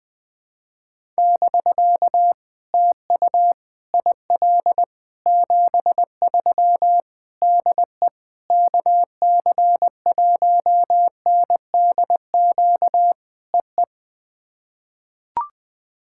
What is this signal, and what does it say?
Morse code, heard as a keyed tone, about 20 wpm, 700 Hz. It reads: <BK> TU IL 73 DE KC1NDQ EE
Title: POTA QSO Practice - 20wpm